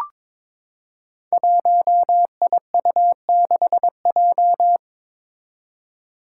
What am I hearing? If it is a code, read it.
Morse code, heard as a keyed tone, about 22 wpm, 700 Hz. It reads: 1IU6J
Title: Five Challenging Letter-Number Combinations 22wpm